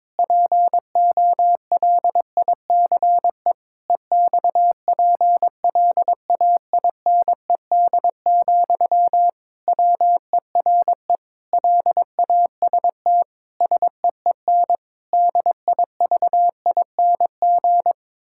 Morse code, heard as a keyed tone, about 21 wpm, 700 Hz. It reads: POLICE EXPLAINED, WERE LAST SEEN DIVING
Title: Pangram Challenge v2 - 22wpm